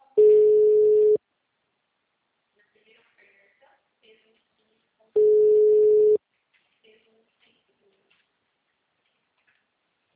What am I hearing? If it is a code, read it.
Ringback tone (Europe, 425 Hz), with speech.